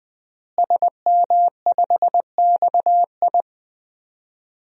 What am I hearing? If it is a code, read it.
Morse code, heard as a keyed tone, about 20 wpm, 700 Hz. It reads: SM5XI
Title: DXCC-100 Call Signs - 5 Characters - Repeated 2x - 20wpm